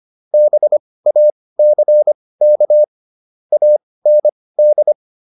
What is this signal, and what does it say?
Morse code, heard as a keyed tone, about 25 wpm, spaced as for 18 wpm, 600 Hz. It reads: BACK AND